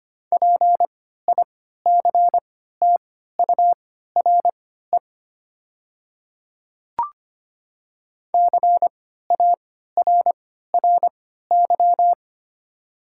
Morse code, heard as a keyed tone, about 25 wpm, spaced as for 14 wpm, 700 Hz. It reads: PICTURE CARRY
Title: 3x Character Spacing - Top 300 Words - 25wpm